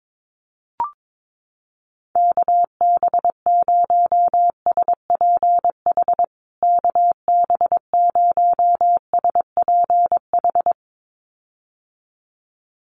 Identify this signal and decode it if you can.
Morse code, heard as a keyed tone, about 22 wpm, 700 Hz. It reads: KB0SPH KB0SPH
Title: USA Call Signs Repeated 2x 22wpm